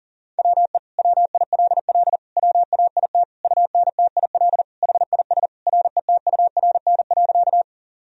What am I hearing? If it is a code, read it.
Morse code, heard as a keyed tone, about 40 wpm, 700 Hz. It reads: WE WILL WAIT UNTIL HIS RETURN.
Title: Sentences from Top 1000 Words - Review Part 1 - 40wpm